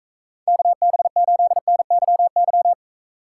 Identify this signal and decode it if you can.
Morse code, heard as a keyed tone, about 41 wpm, 700 Hz. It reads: KB8NYY